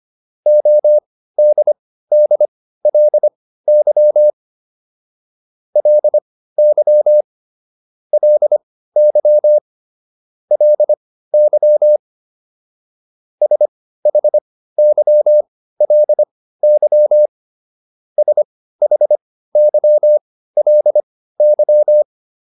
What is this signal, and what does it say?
Morse code, heard as a keyed tone, about 25 wpm, spaced as for 15 wpm, 600 Hz. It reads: ODDLY LY LY LY SHYLY SHYLY